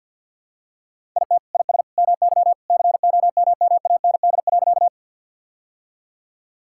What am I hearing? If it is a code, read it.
Morse code, heard as a keyed tone, about 50 wpm, 700 Hz. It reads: IT IS MY COMMAND.
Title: Sentences from Top 1600 Words 50wpm